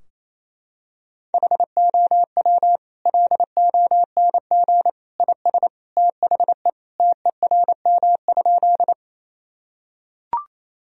Morse code, heard as a keyed tone, about 28 wpm, 700 Hz. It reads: HOW LONG IS THE TERM?
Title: Sentences from Top 1000 Words - Review Part 4 - 28wpm